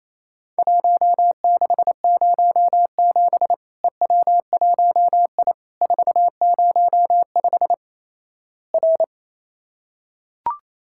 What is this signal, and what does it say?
Morse code, heard as a keyed tone, about 27 wpm, 700 Hz. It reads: 1607 EW1I 405 R
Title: Worked All Europe DX Contest - QTC Exchange - 28wpm